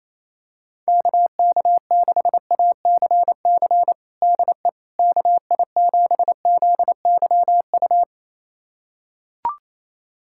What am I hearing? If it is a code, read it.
Morse code, heard as a keyed tone, about 28 wpm, 700 Hz. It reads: KK6ACC DE KI7ZYU
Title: USA Callsign DE Callsign - 6 Characters - 28wpm